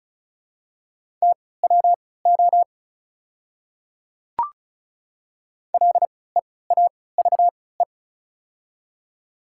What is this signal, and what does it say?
Morse code, heard as a keyed tone, about 36 wpm, spaced as for 20 wpm, 690 Hz. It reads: TWO LEAVE
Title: 3x Character Spacing - Top 500 Words - 35wpm